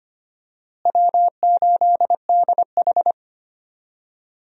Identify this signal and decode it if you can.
Morse code, heard as a keyed tone, about 25 wpm, 700 Hz. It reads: W8DH
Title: USA Call Signs - 4 Characters - Repeated 2x - 25wpm